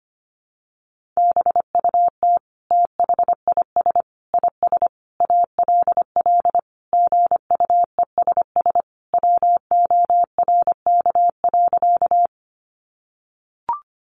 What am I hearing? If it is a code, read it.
Morse code, heard as a keyed tone, about 25 wpm, 700 Hz. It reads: BUT THIS IS ALL GUESS WORK.
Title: Sentences from Top 1500 Words 25wpm